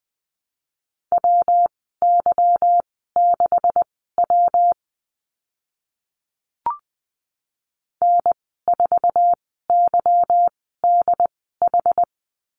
Morse code, heard as a keyed tone, about 20 wpm, spaced as for 15 wpm, 700 Hz. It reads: WY6W N4YDH
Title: 161 Lesson 26g - Letter g - Call Signs 20wpm